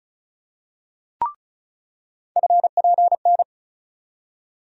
Morse code, heard as a keyed tone, about 34 wpm, 700 Hz. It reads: FPN
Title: USA Call Sign Postfixes - 3 Letters 35wpm